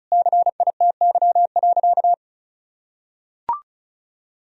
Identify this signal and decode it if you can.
Morse code, heard as a keyed tone, about 36 wpm, 700 Hz. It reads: CITY.
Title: Sentences from Top 1000 Words - Review Part 1 - 35wpm